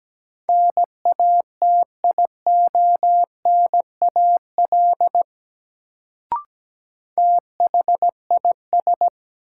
Morse code, heard as a keyed tone, about 17 wpm, 700 Hz. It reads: NATIONAL THIS
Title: Sets of 2 Words - New Word Drill - 17wpm